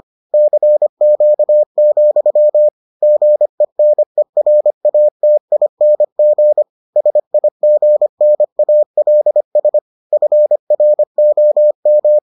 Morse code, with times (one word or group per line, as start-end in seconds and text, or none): 0.3-2.7 CQ,
3.0-6.6 GENERATING
7.0-9.8 SIGNALS
10.1-12.2 FROM